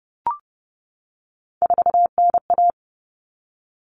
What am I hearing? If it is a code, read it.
Morse code, heard as a keyed tone, about 30 wpm, 700 Hz. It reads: 4NA
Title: Common Cut Numbers - 3-Digit 30wpm